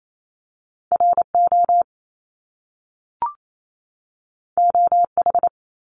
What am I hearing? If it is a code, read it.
Morse code, heard as a keyed tone, about 28 wpm, 700 Hz. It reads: RO OH
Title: Top Two Letter Combinations in Words 28wpm